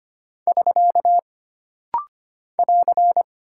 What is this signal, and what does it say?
Morse code, heard as a keyed tone, about 25 wpm, 700 Hz. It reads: <SK> <AR>
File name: Single Punctuation-Prosign - Rapid-Fire - 25wpm